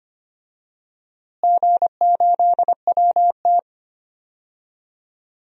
Morse code, heard as a keyed tone, about 25 wpm, 700 Hz. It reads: G8WT
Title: DXCC-100 Call Signs - 4 Characters - Repeated 2x - 25wpm